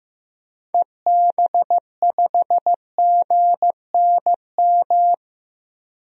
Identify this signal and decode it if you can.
Morse code, heard as a keyed tone, about 15 wpm, 700 Hz. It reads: EB5GNM
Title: DXCC-100 Call Signs - 6 Characters - 15wpm